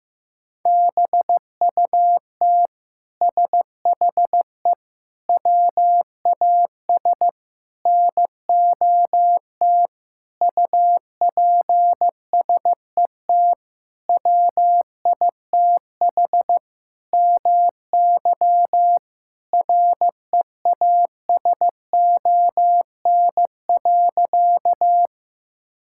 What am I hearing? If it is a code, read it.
Morse code, heard as a keyed tone, about 16 wpm, 700 Hz. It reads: BUT SHE WAS NOT UPSET WITH MY REASON.
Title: Sentences from Top 1000 Words - Review Part 5 - 15wpm